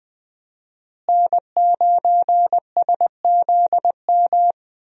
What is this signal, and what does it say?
Morse code, heard as a keyed tone, about 20 wpm, 700 Hz. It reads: N9SZM
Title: USA Call Signs - 5 Characters - Repeated 3x - 20wpm